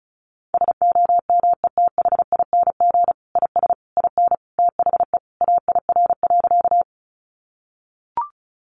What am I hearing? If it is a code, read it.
Morse code, heard as a keyed tone, about 34 wpm, 700 Hz. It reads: SOMETHING IS IN THE AIR.
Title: Sentences from Top 600 Words 35wpm